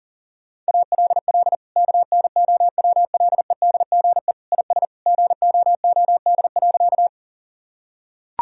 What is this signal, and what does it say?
Morse code, heard as a keyed tone, about 40 wpm, 700 Hz. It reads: ALL KNOWLEDGE IS GOOD.
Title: Sentences from Top 1000 Words - Review Part 3 - 40wpm